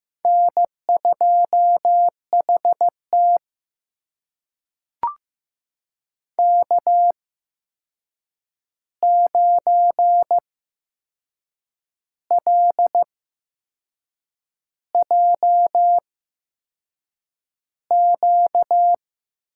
Morse code, heard as a keyed tone, about 15 wpm, spaced as for 6 wpm, 700 Hz. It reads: N2HT K9LJQ
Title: USA Call Signs - 8x Character Spacing - 15wpm